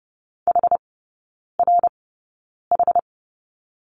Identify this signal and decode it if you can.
Morse code, heard as a keyed tone, about 30 wpm, spaced as for 9 wpm, 700 Hz. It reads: HRH
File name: 081 Lesson 14e - Number 2 - Three Characters 30wpm